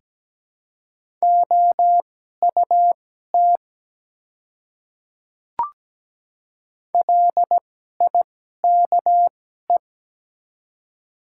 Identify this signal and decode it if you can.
Morse code, heard as a keyed tone, about 17 wpm, spaced as for 12 wpm, 700 Hz. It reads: OUT LIKE
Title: 2x Character Spacing - Top 200 Words - 17wpm